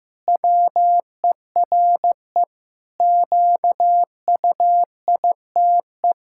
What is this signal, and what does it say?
Morse code, heard as a keyed tone, about 15 wpm, 700 Hz. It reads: WERE QUITE
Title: Pangram Challenge v2 - 15wpm